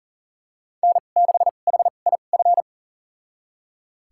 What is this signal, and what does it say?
Morse code, heard as a keyed tone, about 37 wpm, spaced as for 28 wpm, 700 Hz. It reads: N6HIF
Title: USA Call Signs - 5 Characters - 2x Character Spacing - 40wpm